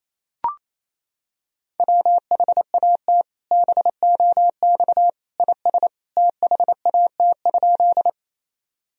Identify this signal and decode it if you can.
Morse code, heard as a keyed tone, about 28 wpm, 700 Hz. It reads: WHAT BOX IS THAT?
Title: Sentences from Top 1000 Words - Review Part 2 - 28wpm